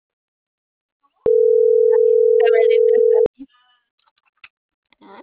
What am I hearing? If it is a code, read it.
Ringback tone (North America), 440 + 480 Hz.